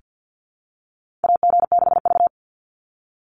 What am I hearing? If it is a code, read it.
Morse code, heard as a keyed tone, about 49 wpm, 700 Hz. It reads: AG6V
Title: NA SOTA Chaser Callsigns 50wpm